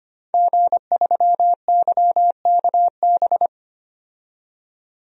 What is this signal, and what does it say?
Morse code, heard as a keyed tone, about 25 wpm, 700 Hz. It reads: G3YKB
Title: DXCC-100 Call Signs - 5 Characters - 25wpm